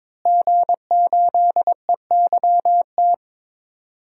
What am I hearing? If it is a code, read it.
Morse code, heard as a keyed tone, about 22 wpm, 700 Hz. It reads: G8EYT
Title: DXCC-100 Call Signs - 5 Characters - Repeated 3x - 22wpm